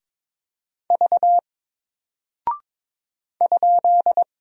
Morse code, heard as a keyed tone, about 23 wpm, 700 Hz. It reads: V ?
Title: Single Character (Letter:Number:Punctuation:Prosign) Alt 22wpm